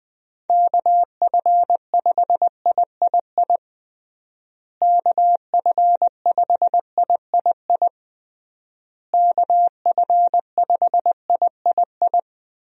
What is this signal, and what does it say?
Morse code, heard as a keyed tone, about 21 wpm, 700 Hz. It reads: KF5III KF5III KF5III
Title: USA Call Signs - 6 Characters - Repeated 3x - 20wpm